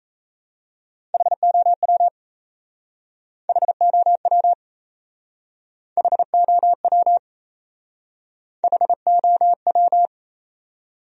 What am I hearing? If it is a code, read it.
Morse code, heard as a keyed tone, about 36 wpm, 700 Hz. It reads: HOW HOW HOW HOW